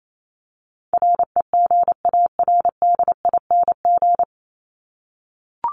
Morse code, heard as a keyed tone, about 28 wpm, 700 Hz. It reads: REGARDING
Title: ING Suffix - Top 100 Words with Base Words 28wpm